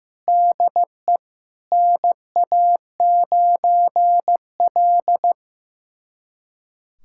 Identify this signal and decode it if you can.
Morse code, heard as a keyed tone, about 16 wpm, 700 Hz. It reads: DE NA9L
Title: USA Callsign DE Callsign - 4 Characters - 15wpm